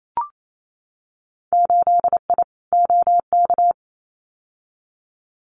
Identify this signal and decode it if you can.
Morse code, heard as a keyed tone, about 28 wpm, 700 Hz. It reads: 8I OK
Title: Winter Field Day Exchanges Encore 28wpm